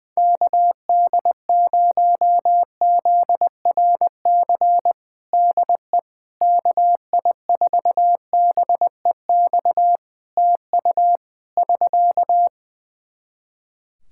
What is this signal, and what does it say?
Morse code, heard as a keyed tone, about 20 wpm, 700 Hz. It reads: KD0ZRC DE KI4BEX TU <SK>
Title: Practice QSO - Exchange 4 - Ending - 20wpm